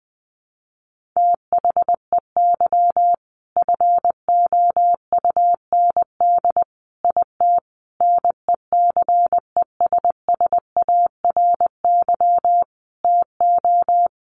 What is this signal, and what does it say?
Morse code, heard as a keyed tone, about 20 wpm, 700 Hz. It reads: THEY FOUND IT NECESSARY TO
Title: Sentences from Top 1400 Words 20wpm